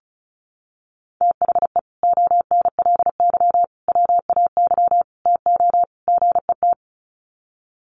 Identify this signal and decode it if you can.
Morse code, heard as a keyed tone, about 34 wpm, 700 Hz. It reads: THE ONLY WAY TO GET
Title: Sets of 5 Words Encore2 35wpm